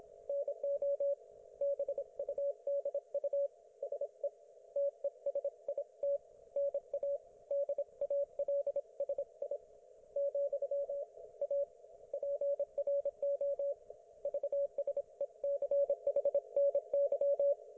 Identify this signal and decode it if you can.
Morse code, heard as a keyed tone, about 26 wpm, 560 Hz. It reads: NO BUDU SE TESIT NA DALSI , A PRO VSECHNY
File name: Archiv / CUC194 / ok1fhi cul194